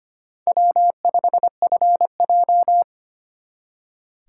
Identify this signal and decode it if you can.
Morse code, heard as a keyed tone, about 33 wpm, 700 Hz. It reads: W5FJ